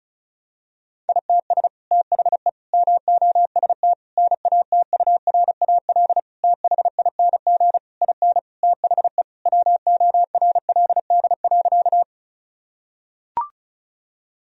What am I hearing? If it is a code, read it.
Morse code, heard as a keyed tone, about 34 wpm, 700 Hz. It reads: ITS THE MOST NATURAL THING IN THE WORLD.
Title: Sentences from Top 1000 Words - Review Part 4 - 35wpm